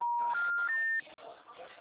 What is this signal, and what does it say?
Special information tone, with speech.
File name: out- -2014